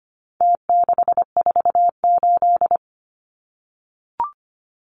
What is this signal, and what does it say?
Morse code, heard as a keyed tone, about 25 wpm, 700 Hz. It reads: T648
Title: Numbers - Contest Style - 4-Digit 25wpm